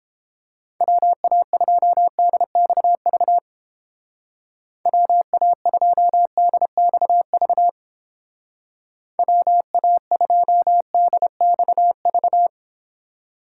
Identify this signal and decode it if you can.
Morse code, heard as a keyed tone, about 30 wpm, 700 Hz. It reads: WA2DXV WA2DXV WA2DXV